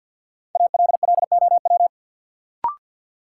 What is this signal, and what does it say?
Morse code, heard as a keyed tone, about 53 wpm, 700 Hz. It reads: ALLOW
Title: Sentences from Top 700 Words - New Word Drill - 50wpm